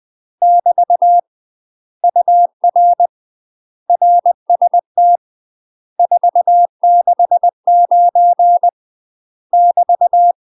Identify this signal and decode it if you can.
Morse code, heard as a keyed tone, about 20 wpm, 700 Hz. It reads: <BT> UR RST 469 <BT>
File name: UK MASTER QSOs TRAINING / 20WPM / UK MASTER QSOs 0051